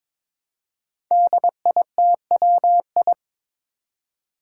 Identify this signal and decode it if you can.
Morse code, heard as a keyed tone, about 22 wpm, 700 Hz. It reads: DITWI
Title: Five Challenging Letter-Number Combinations 22wpm